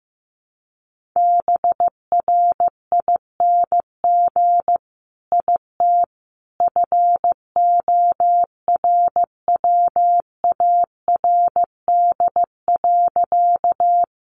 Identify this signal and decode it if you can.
Morse code, heard as a keyed tone, about 15 wpm, 700 Hz. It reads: BRING IT FORWARD.
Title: Sentences from Top 1300 Words 15wpm